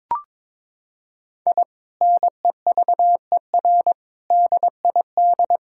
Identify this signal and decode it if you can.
Morse code, heard as a keyed tone, about 22 wpm, 700 Hz. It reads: I NEVER DID
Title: Sentences from Top 1000 Words - Review Part 5 - 22wpm